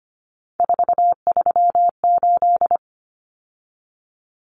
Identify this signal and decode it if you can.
Morse code, heard as a keyed tone, about 25 wpm, 700 Hz. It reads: 438
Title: Common Cut Numbers - 3-Digit 25wpm